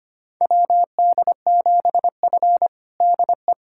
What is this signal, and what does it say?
Morse code, heard as a keyed tone, about 25 wpm, 700 Hz. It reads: WD7F DE
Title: USA Callsign DE Callsign - 4 Characters - 25wpm